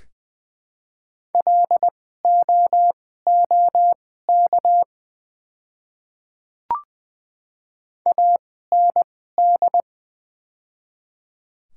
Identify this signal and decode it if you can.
Morse code, heard as a keyed tone, about 20 wpm, spaced as for 14 wpm, 700 Hz. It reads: LOOK AND
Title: 2x Character Spacing - Top 100 Words - 20wpm